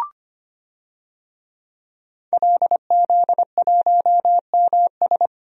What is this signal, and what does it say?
Morse code, heard as a keyed tone, about 25 wpm, 700 Hz. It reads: LZ1MS
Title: DXCC-100 Call Signs - 5 Characters - 25wpm